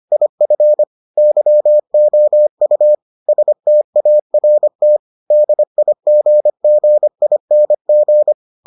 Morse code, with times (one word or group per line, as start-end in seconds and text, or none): 0.1-0.8 IF
1.2-3.0 YOU
3.3-5.0 START
5.3-8.3 DIGGING